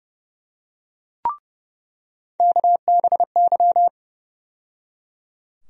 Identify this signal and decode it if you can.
Morse code, heard as a keyed tone, about 30 wpm, 700 Hz. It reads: KBY
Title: USA Call Sign Postfixes - 3 Letters 30wpm